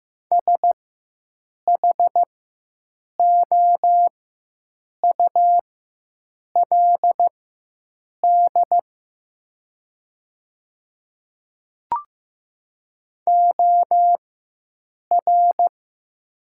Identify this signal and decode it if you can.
Morse code, heard as a keyed tone, about 15 wpm, spaced as for 7 wpm, 700 Hz. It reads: SHOULD OR
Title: 4x Character Spacing - Top 300 Words - 15wpm